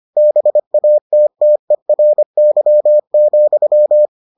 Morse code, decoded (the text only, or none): BATTERY,